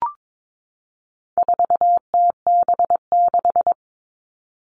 Morse code, heard as a keyed tone, about 22 wpm, 700 Hz. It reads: 4TB6
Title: All Cut Numbers - 4-Digit 22wpm